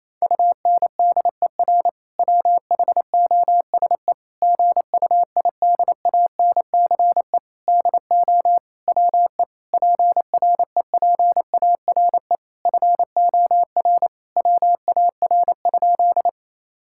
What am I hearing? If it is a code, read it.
Morse code, heard as a keyed tone, about 28 wpm, 700 Hz. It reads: UNDER WHOSE GUIDANCE DO WE PREPARE FOR WAR?
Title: Sentences from Top 1600 Words 28wpm